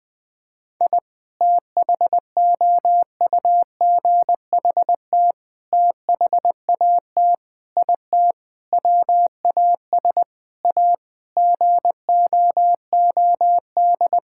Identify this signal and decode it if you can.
Morse code, heard as a keyed tone, about 20 wpm, 700 Hz. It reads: I THOUGHT THAT IT WAS A GOOD
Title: Sentences with 4-Word Phrases - 20wpm